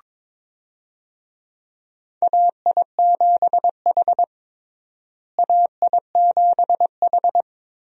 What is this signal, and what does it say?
Morse code, heard as a keyed tone, about 22 wpm, 700 Hz. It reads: AI7H AI7H
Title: USA Call Signs - 4 Characters - Repeated 2x - 22wpm